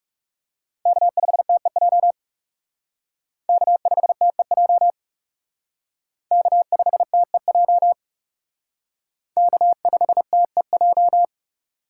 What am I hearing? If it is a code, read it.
Morse code, heard as a keyed tone, about 37 wpm, 700 Hz. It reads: K5TEJ K5TEJ K5TEJ K5TEJ
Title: Speed Racing - USA 1x3 Call Signs 30wpm